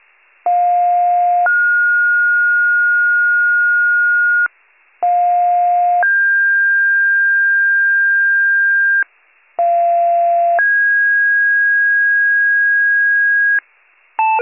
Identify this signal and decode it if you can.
Two-tone page, heard as a one-second tone, then a three-second tone, about 710 then 1430 Hz, about 710 then 1600 Hz, about 690 then 1650 Hz. It is heard from 0.5 to 13.6 s.